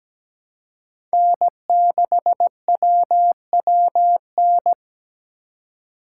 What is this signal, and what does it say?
Morse code, heard as a keyed tone, about 17 wpm, 700 Hz. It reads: N6WWN
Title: USA Call Signs - 5 Characters - 3x Character Spacing - 17wpm